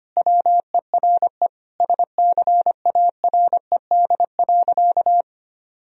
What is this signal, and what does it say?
Morse code, heard as a keyed tone, about 25 wpm, 700 Hz. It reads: WERE SCARED.
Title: Sentences from Top 1000 Words - Review Part 4 - 25wpm